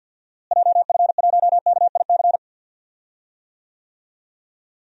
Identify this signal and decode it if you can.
Morse code, heard as a keyed tone, about 51 wpm, 700 Hz. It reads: JF1KIC
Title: DXCC-100 Call Signs - 6 Characters - Repeated 3x - 50wpm